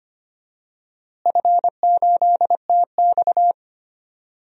Morse code, heard as a keyed tone, about 25 wpm, 700 Hz. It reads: F8TX
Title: DXCC-100 Call Signs - 4 Characters - Repeated 2x - 25wpm